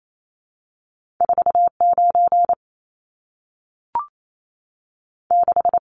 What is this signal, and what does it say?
Morse code, heard as a keyed tone, about 28 wpm, 700 Hz. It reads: 49 6
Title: Numbers - 2-Digit 28wpm